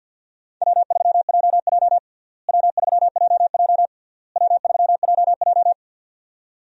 Morse code, heard as a keyed tone, about 52 wpm, 700 Hz. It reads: W3JJ W3JJ W3JJ
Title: USA Call Signs - 4 Characters - Repeated 3x - 50wpm